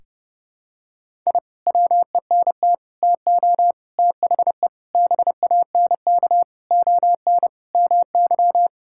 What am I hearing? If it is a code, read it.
Morse code, heard as a keyed tone, about 32 wpm, 700 Hz. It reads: I WENT TO THE BANK ON MY
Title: Sentences from Top 1000 Words - Review Part 6 - 30wpm